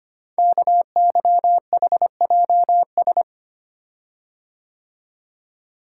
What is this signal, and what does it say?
Morse code, heard as a keyed tone, about 25 wpm, 690 Hz. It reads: KYHJS
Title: Code Groups - 5 Letters - 25wpm